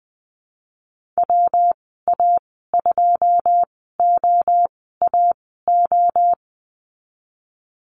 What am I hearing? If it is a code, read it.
Morse code, heard as a keyed tone, about 20 wpm, spaced as for 15 wpm, 700 Hz. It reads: WA2OAO
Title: 122 Lesson 20g - Number 3 - Call Signs 20wpm